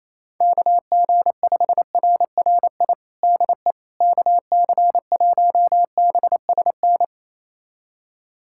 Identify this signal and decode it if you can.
Morse code, heard as a keyed tone, about 28 wpm, 700 Hz. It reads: KG5RRI DE KC1BSN
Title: USA Callsign DE Callsign 28wpm